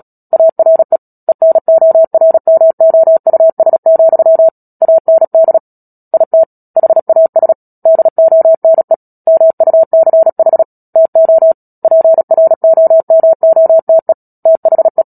Morse code, heard as a keyed tone, about 37 wpm, 650 Hz. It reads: ARE ENORMOUS, AND IT HAS DONE MUCH TO PROMOTE THE
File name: Book Two / 10-Chapter 10 - Epilogue - 36 WPM